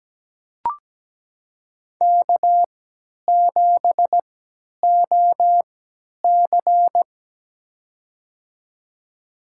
Morse code, heard as a keyed tone, about 17 wpm, spaced as for 10 wpm, 700 Hz. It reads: K7OC
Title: USA Call Signs - 4 Characters - 3x Character Spacing - 17wpm